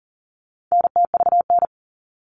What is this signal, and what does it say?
Morse code, heard as a keyed tone, about 40 wpm, 700 Hz. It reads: NTVN
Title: All Cut Numbers - 4-Digit 40wpm